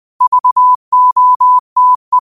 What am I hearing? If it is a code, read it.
Morse code, heard as a keyed tone, about 20 wpm, 1000 Hz. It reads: VOTE
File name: qotdfiles / 20wpm